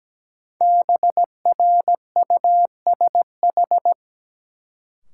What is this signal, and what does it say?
Morse code, heard as a keyed tone, about 17 wpm, 700 Hz. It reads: BRUSH
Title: Sentences from Top 1100 Words - New Word Drill Alt - 17wpm